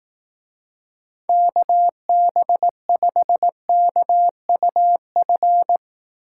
Morse code, heard as a keyed tone, about 18 wpm, 700 Hz. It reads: KB5KUF
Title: Speed Racing - USA 2x3 Call Signs 15wpm